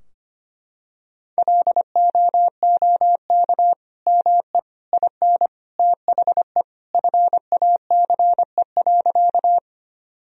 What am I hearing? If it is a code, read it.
Morse code, heard as a keyed tone, about 25 wpm, 700 Hz. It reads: LOOK ME IN THE FACE.
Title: Sentences from Top 1000 Words - Review Part 2 - 25wpm